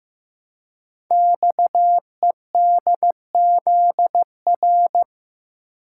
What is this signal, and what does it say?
Morse code, heard as a keyed tone, about 15 wpm, 700 Hz. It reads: XEDZR
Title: Code Groups - 5 Letters - 15wpm